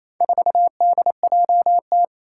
Morse code, heard as a keyed tone, about 28 wpm, 700 Hz. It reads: 4DJT
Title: Four Challenging Letter-Number Combinations 28wpm